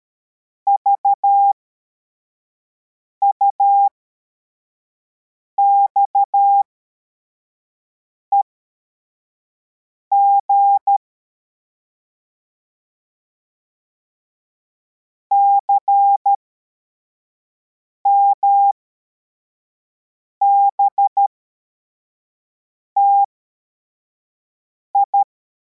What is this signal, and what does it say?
Morse code, heard as a keyed tone, about 13 wpm, spaced as for 4 wpm, 800 Hz. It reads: VUXEG CMBTI
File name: formation / cw / Text53